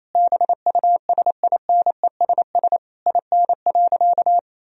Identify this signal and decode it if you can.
Morse code, heard as a keyed tone, about 28 wpm, 700 Hz. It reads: BUSINESS IN.
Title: Sentences from Top 1000 Words - Review Part 3 - 28wpm